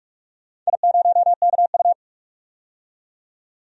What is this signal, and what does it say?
Morse code, heard as a keyed tone, about 46 wpm, 700 Hz. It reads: I0KU
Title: DXCC-100 Call Signs - 4 Characters - 45wpm